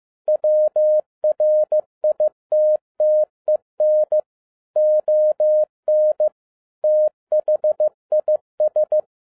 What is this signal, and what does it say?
Morse code, heard as a keyed tone, about 15 wpm, 610 Hz. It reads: WRITTEN ON THIS
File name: 2011-07-mp3 / n47 JOHN K3TN 15wpm